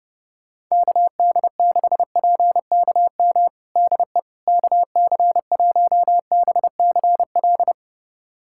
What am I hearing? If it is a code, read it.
Morse code, heard as a keyed tone, about 30 wpm, 700 Hz. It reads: KD6PKM DE KC1BCL
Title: USA Callsign DE Callsign 30wpm